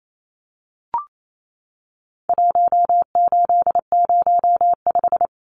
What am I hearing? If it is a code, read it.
Morse code, heard as a keyed tone, about 28 wpm, 700 Hz. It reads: 1805
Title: Numbers - Contest Style - 4-Digit 28wpm